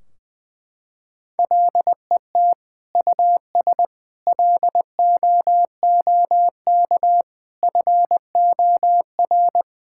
Morse code, heard as a keyed tone, about 21 wpm, 700 Hz. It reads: LET US LOOK FOR
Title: Sentences from Top 1600 Words 20wpm